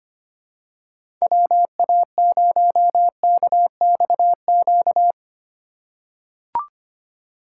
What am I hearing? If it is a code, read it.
Morse code, heard as a keyed tone, about 25 wpm, 700 Hz. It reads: WA0KXQ
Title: USA Call Signs - 3x Character Spacing - 25wpm